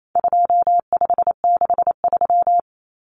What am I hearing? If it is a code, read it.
Morse code, heard as a keyed tone, about 28 wpm, 700 Hz. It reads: 2563
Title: Common Cut Numbers - 4-Digit 28wpm